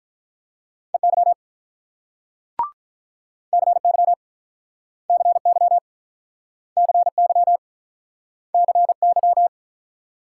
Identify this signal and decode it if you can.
Morse code, heard as a keyed tone, about 49 wpm, 700 Hz. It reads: EY CY CY CY CY